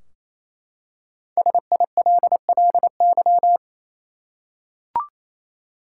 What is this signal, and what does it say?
Morse code, heard as a keyed tone, about 28 wpm, 700 Hz. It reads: SILLY
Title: Five Letter Words - 28wpm